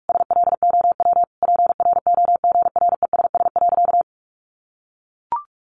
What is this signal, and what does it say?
Morse code, heard as a keyed tone, about 44 wpm, 700 Hz. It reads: SLOW PROGRESS.
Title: Sentences from Top 1400 Words 45wpm